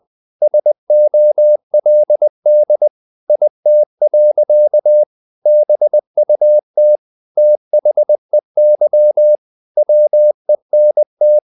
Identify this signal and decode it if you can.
Morse code, heard as a keyed tone, about 20 wpm, 600 Hz. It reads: SOLD IT. BUT THEY WENT
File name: the old man and the sea / the old man and the sea part 1 20wpm